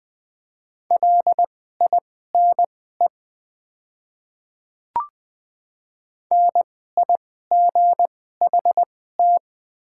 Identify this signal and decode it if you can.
Morse code, heard as a keyed tone, about 20 wpm, spaced as for 14 wpm, 690 Hz. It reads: LINE NIGHT
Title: 2x Character Spacing - Top 300 Words - 20wpm